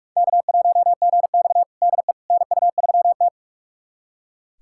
Morse code, heard as a keyed tone, about 44 wpm, 690 Hz. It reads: K1GX DE NU3T
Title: USA Callsign DE Callsign - 4 Characters - 45wpm